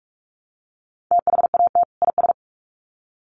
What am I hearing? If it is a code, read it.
Morse code, heard as a keyed tone, about 44 wpm, 700 Hz. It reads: THAT IS
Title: Sets of 2 Words 45wpm